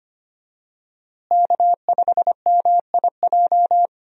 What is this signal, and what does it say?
Morse code, heard as a keyed tone, about 25 wpm, 700 Hz. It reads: K5MIJ
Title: USA Call Signs - 5 Characters - Repeated 3x - 25wpm